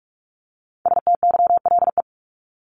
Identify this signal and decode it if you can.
Morse code, heard as a keyed tone, about 44 wpm, 700 Hz. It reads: STYLE
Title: Top 751-1000 Words 45wpm